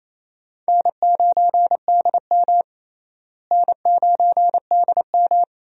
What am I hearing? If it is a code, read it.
Morse code, heard as a keyed tone, about 28 wpm, 700 Hz. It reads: N9DM N9DM
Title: USA Call Signs - 4 Characters - Repeated 2x - 28wpm